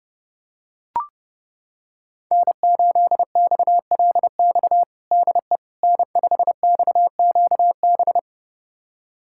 Morse code, heard as a keyed tone, about 30 wpm, 700 Hz. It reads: N8XLX DE N5XQB
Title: USA Callsign DE Callsign - 5 Characters - 30wpm